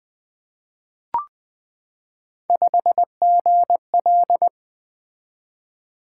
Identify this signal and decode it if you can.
Morse code, heard as a keyed tone, about 20 wpm, 700 Hz. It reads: 5GL
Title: Three Letter-Number Combinations - 20wpm